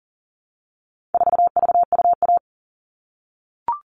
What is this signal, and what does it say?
Morse code, heard as a keyed tone, about 40 wpm, 700 Hz. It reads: 4VUA
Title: All Cut Numbers - 4-Digit 40wpm